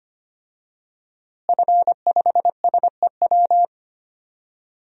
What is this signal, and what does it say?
Morse code, heard as a keyed tone, about 25 wpm, 700 Hz. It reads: F5SEW
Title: DXCC-100 Call Signs - 5 Characters - Repeated 2x - 25wpm